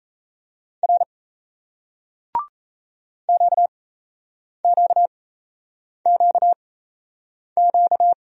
Morse code, heard as a keyed tone, about 37 wpm, spaced as for 8 wpm, 700 Hz. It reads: R QQQQ